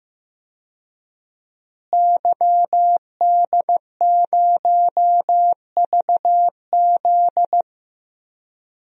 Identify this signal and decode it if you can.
Morse code, heard as a keyed tone, about 15 wpm, 700 Hz. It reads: YD0VZ
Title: DXCC-100 Call Signs - 5 Characters - Repeated 2x - 15wpm